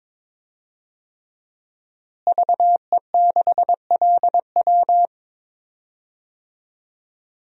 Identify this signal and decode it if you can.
Morse code, heard as a keyed tone, about 22 wpm, 700 Hz. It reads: VE6LW
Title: DXCC-100 Call Signs - 5 Characters - Repeated 3x - 22wpm